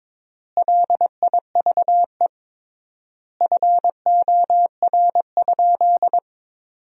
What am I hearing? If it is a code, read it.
Morse code, heard as a keyed tone, about 22 wpm, 700 Hz. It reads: LIVE FOR?
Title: 3x Word-Spacing - Sentences from Top 200 Words - 22wpm